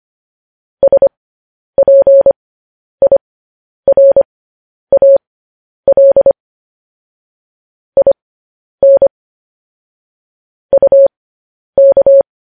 Morse code, heard as a keyed tone, about 25 wpm, spaced as for 10 wpm, 550 Hz. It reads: SPIRAL IN UK
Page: news-headlines-in-morse-code-at-10-wpm